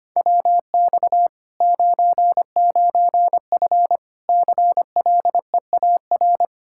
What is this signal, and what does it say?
Morse code, heard as a keyed tone, about 25 wpm, 700 Hz. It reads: WX 99F CLEAR
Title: Practice QSO - Exchange 2 - Info - 25wpm